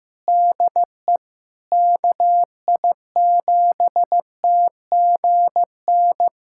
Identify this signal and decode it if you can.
Morse code, heard as a keyed tone, about 15 wpm, 700 Hz. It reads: DE KI7TGN
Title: USA Callsign DE Callsign - 6 Characters - 15wpm